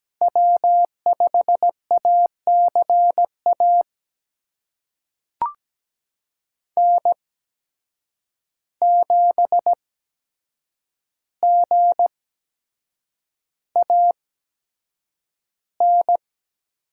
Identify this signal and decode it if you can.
Morse code, heard as a keyed tone, about 17 wpm, spaced as for 7 wpm, 700 Hz. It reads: W5ACA N7GAN
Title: USA Call Signs - 5 Characters - 8x Character Spacing - 17wpm